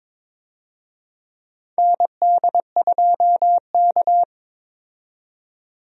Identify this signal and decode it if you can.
Morse code, heard as a keyed tone, about 22 wpm, 700 Hz. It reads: ND2K
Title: USA Call Signs - 4 Characters - 22wpm